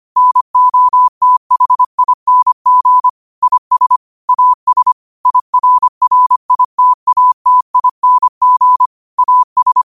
Morse code, heard as a keyed tone, about 25 wpm, 1000 Hz. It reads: NOTHING IS AS IRRITATING AS